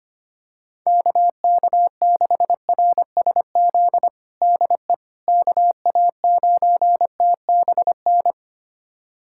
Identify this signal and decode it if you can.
Morse code, heard as a keyed tone, about 25 wpm, 700 Hz. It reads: KK6RSZ DE KA9TBN
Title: USA Callsign DE Callsign 25wpm